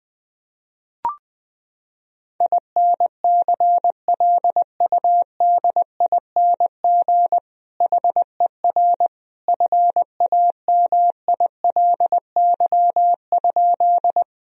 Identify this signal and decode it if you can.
Morse code, heard as a keyed tone, about 20 wpm, 700 Hz. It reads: INCLUDING HER FAMILY?
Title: Sentences from Top 1000 Words - Review Part 4 - 20wpm